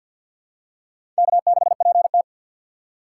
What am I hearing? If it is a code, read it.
Morse code, heard as a keyed tone, about 52 wpm, 700 Hz. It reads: K6PT
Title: USA Call Signs - 4 Characters - 3x Character Spacing - 50wpm